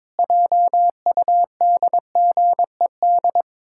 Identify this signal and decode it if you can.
Morse code, heard as a keyed tone, about 22 wpm, 700 Hz. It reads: JUDGED
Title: Pangram Challenge v2 - 22wpm